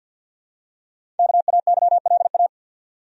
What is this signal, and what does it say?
Morse code, heard as a keyed tone, about 47 wpm, 700 Hz. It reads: KAYLA
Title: USA Common Names - 50wpm